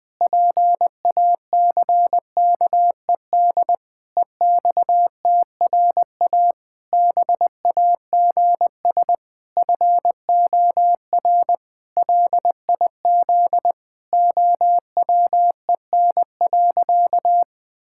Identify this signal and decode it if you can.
Morse code, heard as a keyed tone, about 21 wpm, 700 Hz. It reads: PACKED EXTRA BAGS FOR LIZ OWEN.
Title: Pangram Challenge v2 - 20wpm